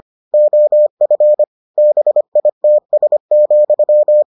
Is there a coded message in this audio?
Yes: Morse code from 0.3 to 4.2 s, heard as a keyed tone, about 25 wpm, 600 Hz. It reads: OF BITS,